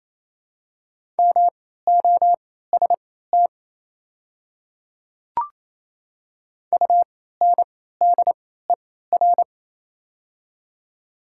Morse code, heard as a keyed tone, about 27 wpm, spaced as for 16 wpm, 690 Hz. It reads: MOST UNDER
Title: 3x Character Spacing - Top 200 Words - 28wpm